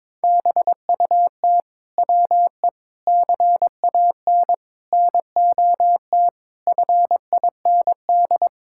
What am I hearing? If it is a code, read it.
Morse code, heard as a keyed tone, about 22 wpm, 700 Hz. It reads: BUT WE CAN NOT FIND
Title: Sentences from Top 1000 Words - Review Part 5 - 22wpm